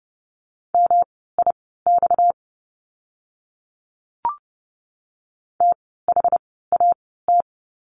Morse code, heard as a keyed tone, about 30 wpm, spaced as for 17 wpm, 700 Hz. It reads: MIX THAT
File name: course / 30:10wpm / 230 Lesson 37f - Letter x - Words 30wpm